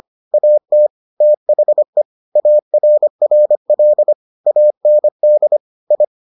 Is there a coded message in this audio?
Yes: Morse code from 0.3 to 6.0 s, heard as a keyed tone, about 25 wpm, 600 Hz. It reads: AT THE ARRL AND I